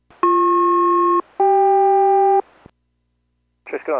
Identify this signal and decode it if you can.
Aircraft SELCAL, heard as two pairs of tones, with speech. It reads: BP-CK